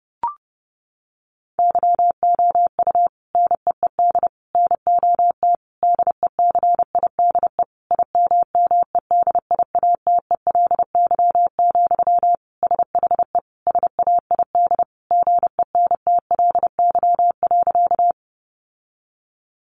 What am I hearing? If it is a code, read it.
Morse code, heard as a keyed tone, about 30 wpm, 700 Hz. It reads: YOU NEED NOT DECIDE IMMEDIATELY, SHE SAID GENTLY.
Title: Sentences from Top 1500 Words 30wpm